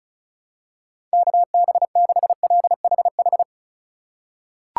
Morse code, heard as a keyed tone, about 36 wpm, 700 Hz. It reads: KB6LHH
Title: USA Call Signs - 6 Characters - 3x Character Spacing - 35wpm